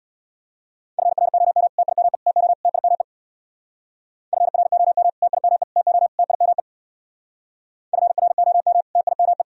Morse code, heard as a keyed tone, about 59 wpm, 700 Hz. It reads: FROM TIME TO TIME FROM TIME TO TIME FROM TIME
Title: Speed-Racing - Sets of 4 Words 50wpm